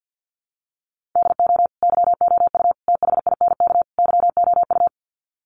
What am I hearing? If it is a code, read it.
Morse code, heard as a keyed tone, about 51 wpm, 700 Hz. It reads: DO YOU THINK YOU
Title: Sets of 4 Words 50wpm